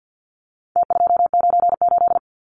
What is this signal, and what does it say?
Morse code, heard as a keyed tone, about 50 wpm, 700 Hz. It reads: T298
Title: Numbers - Contest Style - 4-Digit 50wpm